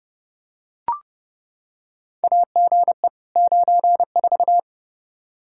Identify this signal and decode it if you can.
Morse code, heard as a keyed tone, about 29 wpm, 700 Hz. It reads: AGE 94
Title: Practice QSO - Exchange 3 - More Info - 30wpm